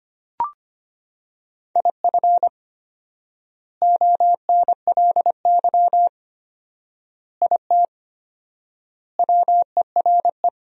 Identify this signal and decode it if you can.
Morse code, heard as a keyed tone, about 25 wpm, 700 Hz. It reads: IF ONLY IT WERE
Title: 4x Word-Spacing - Sentences from Top 500 Words - 25wpm